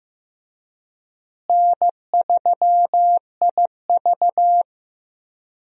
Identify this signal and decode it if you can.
Morse code, heard as a keyed tone, about 15 wpm, 700 Hz. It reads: N3IV
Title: USA Call Signs - 4 Characters - Repeated 2x - 15wpm